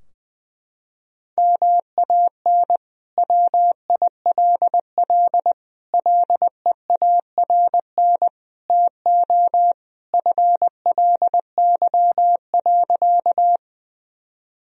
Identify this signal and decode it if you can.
Morse code, heard as a keyed tone, about 21 wpm, 700 Hz. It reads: MAN WILL LEARN TO FLY.
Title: Sentences from Top 1000 Words - Review Part 2 - 20wpm